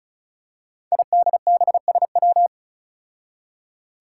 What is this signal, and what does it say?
Morse code, heard as a keyed tone, about 36 wpm, 700 Hz. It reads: IDBSW
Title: Five Challenging Letter-Number Combinations 35wpm